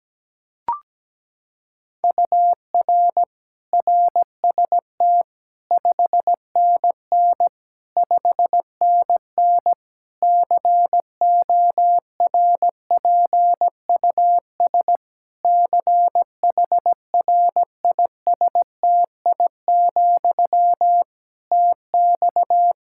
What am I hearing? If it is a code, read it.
Morse code, heard as a keyed tone, about 17 wpm, 700 Hz. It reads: UR RST 5NN 5NN CORPUS CHRISTI, TX
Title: K1USN SST QSO Practice 17wpm